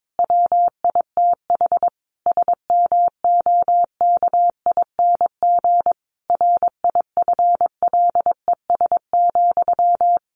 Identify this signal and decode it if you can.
Morse code, heard as a keyed tone, about 22 wpm, 700 Hz. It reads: WITH SMOKING RIFLES,
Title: A Sound Of Thunder 22wpm